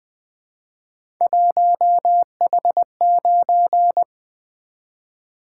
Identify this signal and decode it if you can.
Morse code, heard as a keyed tone, about 20 wpm, 700 Hz. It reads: 1H9
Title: Three Letter-Number Combinations - 20wpm